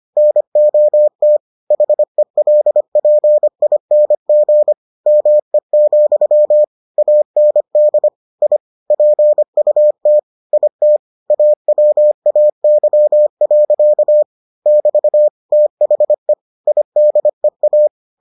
Morse code, heard as a keyed tone, about 25 wpm, 600 Hz. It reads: NOT HELPING ME, AND I PUT IT AWAY. <BT> THE IDEA